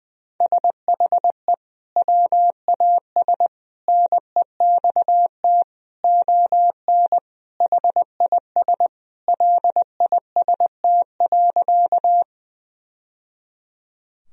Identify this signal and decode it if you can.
Morse code, heard as a keyed tone, about 19 wpm, 700 Hz. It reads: SHE WAS NEXT ON HIS LIST.
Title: Sentences from Top 1000 Words - Review Part 2 - 20wpm